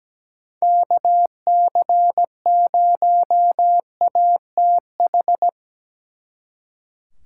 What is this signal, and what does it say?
Morse code, heard as a keyed tone, about 19 wpm, 700 Hz. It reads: KC0ATH
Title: Speed Racing - USA 2x3 Call Signs 17wpm